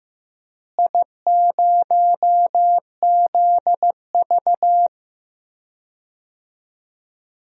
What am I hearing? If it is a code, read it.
Morse code, heard as a keyed tone, about 15 wpm, 700 Hz. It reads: I0ZV
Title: DXCC-100 Call Signs - 4 Characters - Repeated 2x - 15wpm